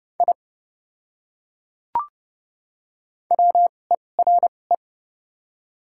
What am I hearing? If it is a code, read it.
Morse code, heard as a keyed tone, about 29 wpm, spaced as for 21 wpm, 700 Hz. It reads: I WERE
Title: 2x Character Spacing - Top 100 Words - 30wpm